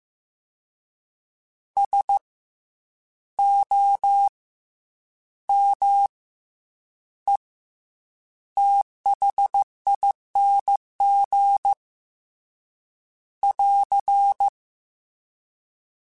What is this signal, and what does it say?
Morse code, heard as a keyed tone, about 15 wpm, 800 Hz. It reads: S O M E THING <AR>
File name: Special INGs / Special INGs-mp3s / SlowToFaster ing @13